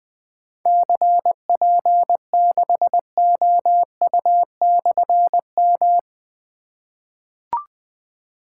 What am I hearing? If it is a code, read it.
Morse code, heard as a keyed tone, about 20 wpm, 700 Hz. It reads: CP6OU/M
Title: Difficult Call Signs 20wpm